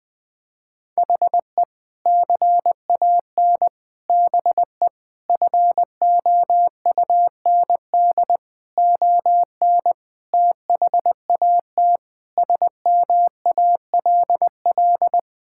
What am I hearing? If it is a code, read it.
Morse code, heard as a keyed tone, about 19 wpm, 700 Hz. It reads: HE CAN BE FOUND ON THAT SMALL
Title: Sentences from Top 1600 Words 20wpm